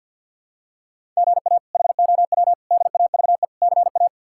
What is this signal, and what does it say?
Morse code, heard as a keyed tone, about 46 wpm, 690 Hz. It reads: GA HOW DAVE CA
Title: K1USN SST QSO Practice - 50wpm